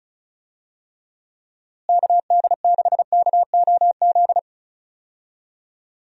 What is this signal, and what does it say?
Morse code, heard as a keyed tone, about 34 wpm, 700 Hz. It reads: KD6KOZ
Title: USA Call Signs - 6 Characters - Repeated 2x - 35wpm